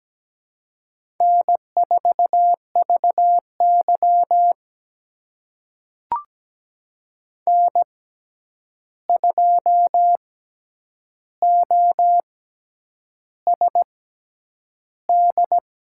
Morse code, heard as a keyed tone, about 17 wpm, spaced as for 8 wpm, 700 Hz. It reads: N4VY N2OSD
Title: USA Call Signs - 6x Character Spacing - 17wpm